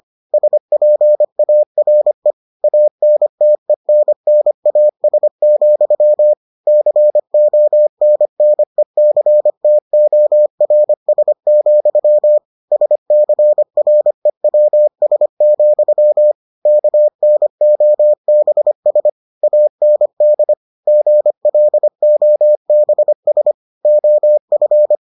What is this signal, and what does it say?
Morse code, heard as a keyed tone, about 25 wpm, 600 Hz. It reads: SPARE ANTENNAS, CONNECTORS, SCREWS, KNOBS AND GLOBS OF